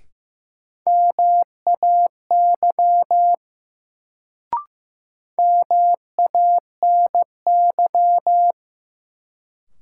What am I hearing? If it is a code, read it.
Morse code, heard as a keyed tone, about 15 wpm, 700 Hz. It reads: MAY MANY
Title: Sets of 2 Words - New Word Drill - 15wpm